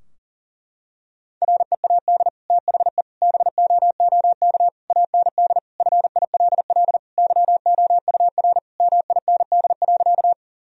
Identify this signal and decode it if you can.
Morse code, heard as a keyed tone, about 40 wpm, 700 Hz. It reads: READ THE BOOK AND FILL YOUR MIND.
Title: Sentences from Top 1000 Words - Review Part 3 - 40wpm